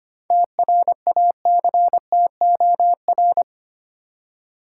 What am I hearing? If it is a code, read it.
Morse code, heard as a keyed tone, about 25 wpm, 700 Hz. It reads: TRACTOR
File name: practice / 25wpm / Easy Words with Hints - 25wpm